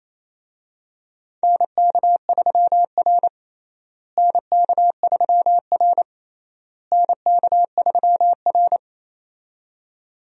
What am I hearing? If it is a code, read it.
Morse code, heard as a keyed tone, about 28 wpm, 700 Hz. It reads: NK3R NK3R NK3R
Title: USA Call Signs - 4 Characters - Repeated 3x - 28wpm